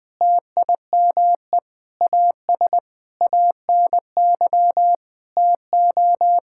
Morse code, heard as a keyed tone, about 21 wpm, 700 Hz. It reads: TIME AS ANY TO
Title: Sentences with 7-Word Phrases - Spoken Beginning - 20wpm